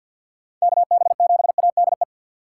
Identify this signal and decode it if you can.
Morse code, heard as a keyed tone, about 49 wpm, 690 Hz. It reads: KB7ADE
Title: USA Call Signs - 4x Character Spacing - 50wpm